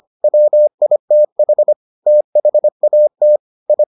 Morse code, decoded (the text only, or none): WITH THAT I